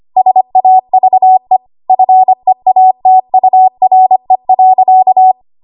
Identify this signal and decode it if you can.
Morse code, heard as a keyed tone, about 25 wpm, 750 Hz. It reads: SAVE FEATURE.
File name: Archive / 25 WPM / 25WPM